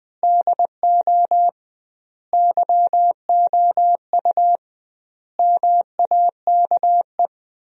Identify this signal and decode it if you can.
Morse code, heard as a keyed tone, about 21 wpm, 700 Hz. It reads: DO YOU MAKE
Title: 2x Word-Spacing - Sentences from Top 100 Words - 20wpm